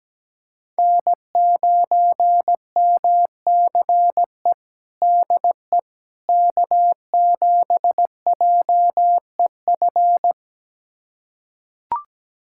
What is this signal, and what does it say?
Morse code, heard as a keyed tone, about 17 wpm, 700 Hz. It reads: N9MCE DE K7JEF
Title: USA Callsign DE Callsign - 5 Characters - 17wpm